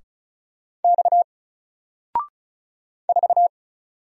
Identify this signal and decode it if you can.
Morse code, heard as a keyed tone, about 36 wpm, 700 Hz. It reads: X 4
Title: Single Letter-Number Alt 35wpm